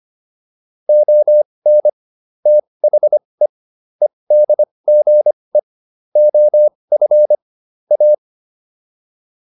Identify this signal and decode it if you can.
Morse code, heard as a keyed tone, about 25 wpm, spaced as for 20 wpm, 600 Hz. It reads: ON THE EDGE OF A